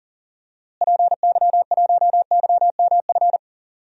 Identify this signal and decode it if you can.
Morse code, heard as a keyed tone, about 40 wpm, 700 Hz. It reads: PY1YMF
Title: DXCC-100 Call Signs - 6 Characters - Repeated 2x - 40wpm